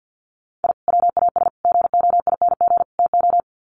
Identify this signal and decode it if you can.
Morse code, heard as a keyed tone, about 50 wpm, 700 Hz. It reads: I WAS GOING TO
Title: Sets of 4 Words 50wpm